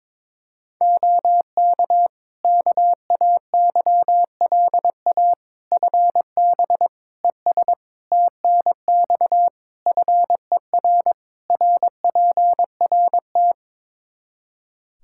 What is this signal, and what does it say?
Morse code, heard as a keyed tone, about 22 wpm, 700 Hz. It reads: OK KAYLA FB ES TNX FER RPRT
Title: K1USN SST QSO Practice 22wpm